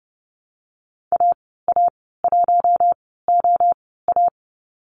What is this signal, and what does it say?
Morse code, heard as a keyed tone, about 30 wpm, spaced as for 17 wpm, 700 Hz. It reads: AA1OA
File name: Lesson 14g - Number 2 - Call Signs 30wpm